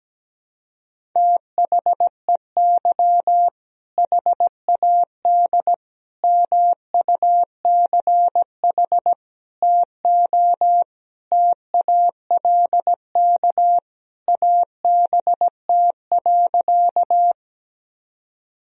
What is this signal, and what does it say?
Morse code, heard as a keyed tone, about 17 wpm, 700 Hz. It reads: THEY HAD MUCH TO TALK ABT.
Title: Sentences from Top 400 Words and Abbreviations 17wpm